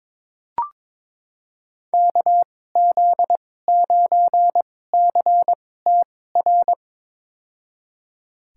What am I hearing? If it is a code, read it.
Morse code, heard as a keyed tone, about 22 wpm, spaced as for 16 wpm, 700 Hz. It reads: KZ9CTR
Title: USA Call Signs - 6 Characters - 2x Character Spacing - 22wpm